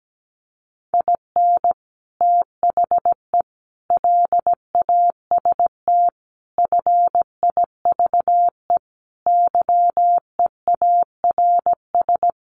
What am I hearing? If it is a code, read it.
Morse code, heard as a keyed tone, about 17 wpm, 700 Hz. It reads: IN THE LAST FIVE YEARS
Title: Sets of 5 Words Encore 17wpm